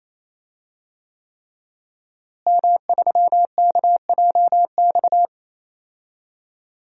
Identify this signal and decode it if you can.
Morse code, heard as a keyed tone, about 28 wpm, 700 Hz. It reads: M3KJX
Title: DXCC-100 Call Signs - 5 Characters - Repeated 3x - 28wpm